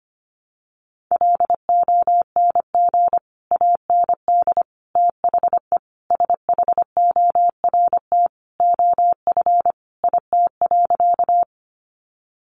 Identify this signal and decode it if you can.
Morse code, heard as a keyed tone, about 25 wpm, 700 Hz. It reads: LONG AND THE SHORT OF IT.
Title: Sentences from Top 400 Words 25wpm